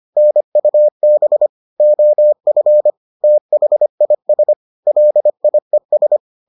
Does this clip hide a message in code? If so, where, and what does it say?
Yes: Morse code from 0.2 to 6.2 s, heard as a keyed tone, about 25 wpm, 600 Hz. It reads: NUB OF THIS LIES